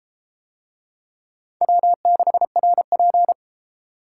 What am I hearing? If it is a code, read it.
Morse code, heard as a keyed tone, about 31 wpm, 700 Hz. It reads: W6RP